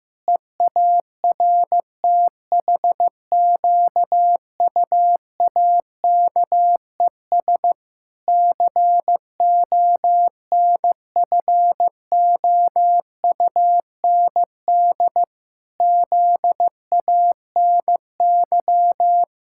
Morse code, heard as a keyed tone, about 14 wpm, 700 Hz. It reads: EARTHQUAKES CONFOUND ZANY
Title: Pangram Challenge v2 - 15wpm